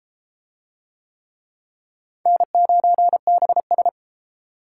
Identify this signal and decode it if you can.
Morse code, heard as a keyed tone, about 32 wpm, 700 Hz. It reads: N9BS